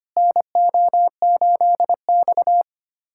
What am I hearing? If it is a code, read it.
Morse code, heard as a keyed tone, about 25 wpm, 700 Hz. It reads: NO8X
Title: USA Callsign DE Callsign - 4 Characters - 25wpm